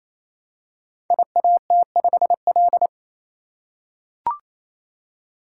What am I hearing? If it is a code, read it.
Morse code, heard as a keyed tone, about 29 wpm, 700 Hz. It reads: IAT5L
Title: Code Groups - 5 Letter:Numbers - 28wpm